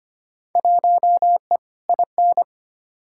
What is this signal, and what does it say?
Morse code, heard as a keyed tone, about 25 wpm, 700 Hz. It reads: 1E IN
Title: Field Day Exchanges Encore - 25wpm